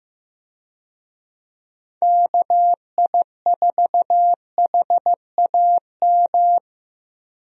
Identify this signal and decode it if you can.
Morse code, heard as a keyed tone, about 15 wpm, 700 Hz. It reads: KI4HAM
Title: USA Call Signs - 6 Characters - 15wpm